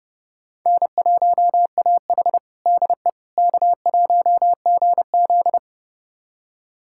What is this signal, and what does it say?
Morse code, heard as a keyed tone, about 30 wpm, 700 Hz. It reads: N1AH DE K1GZ
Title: USA Callsign DE Callsign - 4 Characters - 30wpm